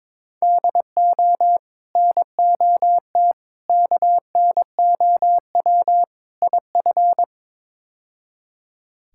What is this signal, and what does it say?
Morse code, heard as a keyed tone, about 21 wpm, 700 Hz. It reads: DO NOT KNOW IF
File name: practice / 22wpm / Build A Phrase - 5-Word Phrases - 22wpm